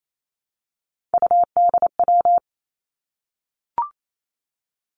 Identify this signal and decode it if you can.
Morse code, heard as a keyed tone, about 28 wpm, 700 Hz. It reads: UDW
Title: Three Challenging Letter-Number Combinations 28wpm